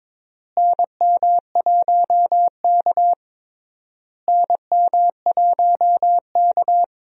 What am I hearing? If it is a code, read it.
Morse code, heard as a keyed tone, about 22 wpm, 700 Hz. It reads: NM1K NM1K
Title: USA Call Signs - 4 Characters - Repeated 3x - 22wpm